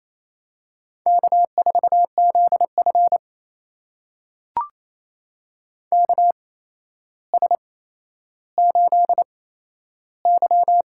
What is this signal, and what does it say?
Morse code, heard as a keyed tone, about 28 wpm, spaced as for 12 wpm, 700 Hz. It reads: K4ZF KS8Y
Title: USA Call Signs - 4 Characters - 8x Character Spacing - 28wpm